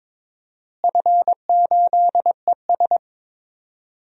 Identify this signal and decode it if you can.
Morse code, heard as a keyed tone, about 22 wpm, 700 Hz. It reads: F8ES
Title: DXCC-100 Call Signs - 4 Characters - Repeated 2x - 22wpm